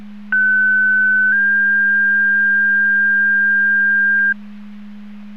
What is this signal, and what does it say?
Two-tone page, heard as a one-second tone, then a three-second tone, about 1500 then 1600 Hz. It is heard from 0.3 to 4.3 s.
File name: Portage 154.385 Paging